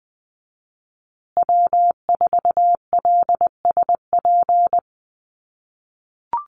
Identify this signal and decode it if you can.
Morse code, heard as a keyed tone, about 20 wpm, 700 Hz. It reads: W4LSP
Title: USA Call Signs Repeated 2x 20wpm